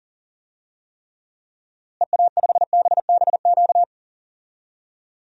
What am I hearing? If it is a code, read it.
Morse code, heard as a keyed tone, about 40 wpm, 700 Hz. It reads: EA5BBQ
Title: DXCC-100 Call Signs - 6 Characters - Repeated 2x - 40wpm